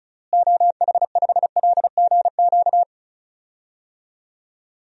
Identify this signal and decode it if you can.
Morse code, heard as a keyed tone, about 34 wpm, 700 Hz. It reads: OH5LGQ
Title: DXCC-100 Call Signs - 6 Characters - Repeated 2x - 35wpm